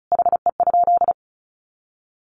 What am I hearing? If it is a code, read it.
Morse code, heard as a keyed tone, about 34 wpm, 700 Hz. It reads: HE?
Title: Sentences from Top 1300 Words 35wpm